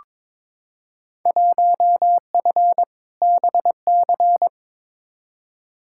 Morse code, heard as a keyed tone, about 21 wpm, 700 Hz. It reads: 1F BC
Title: Field Day Exchanges Encore - 22wpm